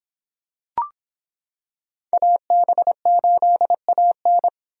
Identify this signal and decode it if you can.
Morse code, heard as a keyed tone, about 25 wpm, 700 Hz. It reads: AB8AN